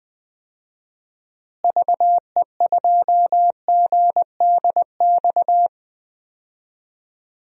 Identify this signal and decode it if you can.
Morse code, heard as a keyed tone, about 19 wpm, 700 Hz. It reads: VE2GDX
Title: DXCC-100 Call Signs - 6 Characters - Repeated 3x - 20wpm